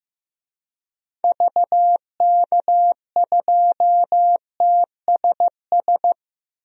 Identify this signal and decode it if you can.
Morse code, heard as a keyed tone, about 15 wpm, 700 Hz. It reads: VK2TSS
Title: DXCC-100 Call Signs - 6 Characters - Repeated 2x - 15wpm